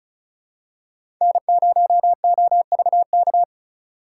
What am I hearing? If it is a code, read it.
Morse code, heard as a keyed tone, about 34 wpm, 700 Hz. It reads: N0OVK
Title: USA Call Signs - 5 Characters - 4x Character Spacing - 35wpm